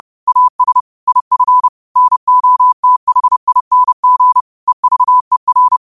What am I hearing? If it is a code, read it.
Morse code, heard as a keyed tone, about 30 wpm, 1000 Hz. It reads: AS IF NOTHING EVER
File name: qotdfiles / 30wpm